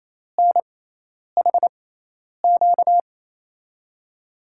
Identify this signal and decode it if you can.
Morse code, heard as a keyed tone, about 28 wpm, spaced as for 10 wpm, 700 Hz. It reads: NHQ
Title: USA Call Signs - 6x Character Spacing - 28wpm